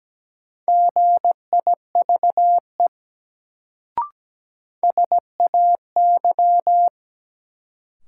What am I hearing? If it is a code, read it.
Morse code, heard as a keyed tone, about 17 wpm, 700 Hz. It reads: GIVE SAY
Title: Sentences from Top 200 Words - New Word Drill - 17wpm